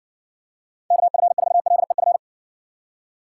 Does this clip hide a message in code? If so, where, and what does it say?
Yes: Morse code from 0.9 to 2.2 s, heard as a keyed tone, about 60 wpm, 700 Hz. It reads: KF4LEF